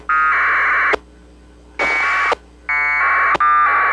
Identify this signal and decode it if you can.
Packet radio (AX.25, as in APRS).
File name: SignalSamples / aprs Hz